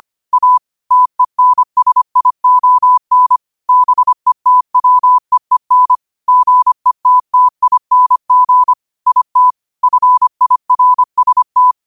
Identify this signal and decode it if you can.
Morse code, heard as a keyed tone, about 25 wpm, 1000 Hz. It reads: A TENSION BETWEEN GETTING IT FIRST
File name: qotdfiles / 25wpm